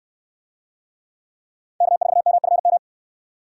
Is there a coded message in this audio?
Yes: Morse code from 1.8 to 2.8 s, heard as a keyed tone, about 63 wpm, 700 Hz. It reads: K4MFM